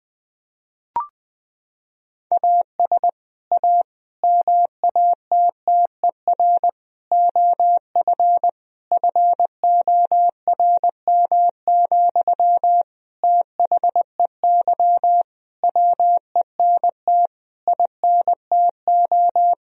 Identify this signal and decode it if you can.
Morse code, heard as a keyed tone, about 20 wpm, 700 Hz. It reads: AS A MATTER OF FORM, THEY WENT INTO
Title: Sentences from Top 1000 Words - Review Part 1 - 20wpm